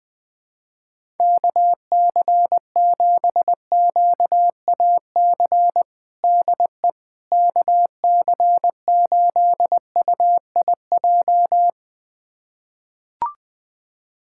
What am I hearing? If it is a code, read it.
Morse code, heard as a keyed tone, about 20 wpm, 700 Hz. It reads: KC7QAC DE KC8UIJ
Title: USA Callsign DE Callsign - 6 Characters - 20wpm